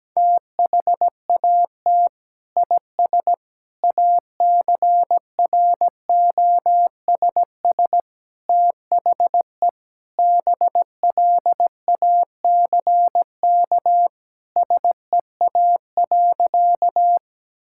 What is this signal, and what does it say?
Morse code, heard as a keyed tone, about 17 wpm, 700 Hz. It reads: THAT IS ACROSS THE BLACK SEA.
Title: Sentences from Top 1000 Words - Review Part 5 - 17wpm